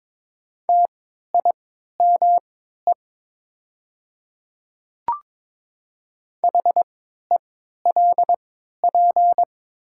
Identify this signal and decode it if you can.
Morse code, heard as a keyed tone, about 22 wpm, spaced as for 13 wpm, 700 Hz. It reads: TIME HELP
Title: 3x Character Spacing - Top 300 Words - 22wpm